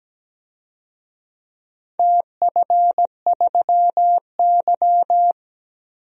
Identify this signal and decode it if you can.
Morse code, heard as a keyed tone, about 17 wpm, 700 Hz. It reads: TF3Y
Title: DXCC-100 Call Signs - 4 Characters - 17wpm